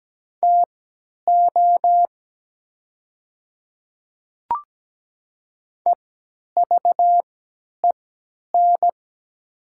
Morse code, heard as a keyed tone, about 17 wpm, spaced as for 10 wpm, 700 Hz. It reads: TO EVEN
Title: 3x Character Spacing - Top 200 Words - 17wpm